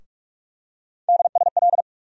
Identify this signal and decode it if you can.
Morse code, heard as a keyed tone, about 46 wpm, 700 Hz. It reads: DSL
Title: USA Call Sign Postfixes - 3 Letters 45wpm